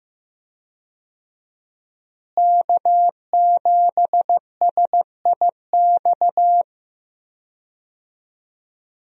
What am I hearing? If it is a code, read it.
Morse code, heard as a keyed tone, about 15 wpm, 700 Hz. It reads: K7SIX
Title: USA Call Signs - 5 Characters - Repeated 2x - 15wpm